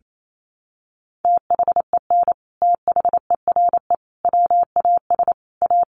Morse code, heard as a keyed tone, about 28 wpm, 700 Hz. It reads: THEN THERE WAS A
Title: Sentences from Top 1100 Words 28wpm